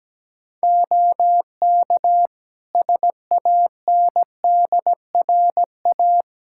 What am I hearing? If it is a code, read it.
Morse code, heard as a keyed tone, about 17 wpm, 700 Hz. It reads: OK SANDRA
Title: Practice QSO - Exchange 3 - More Info - 17wpm